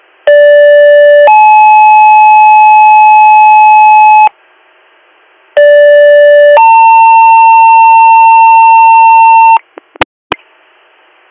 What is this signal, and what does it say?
Two-tone page, heard as a one-second tone, then a three-second tone, about 580 then 860 Hz, about 580 then 900 Hz. It is heard from 0.3 to 9.6 s.